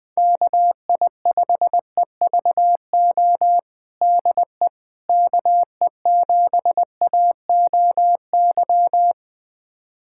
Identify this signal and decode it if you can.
Morse code, heard as a keyed tone, about 20 wpm, 700 Hz. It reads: KI5EVO DE KE7AOY
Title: USA Callsign DE Callsign 20wpm